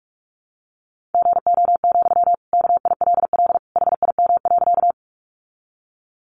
Morse code, heard as a keyed tone, about 44 wpm, 700 Hz. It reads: GO, KILL HIM.
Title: Sentences from Top 1400 Words 45wpm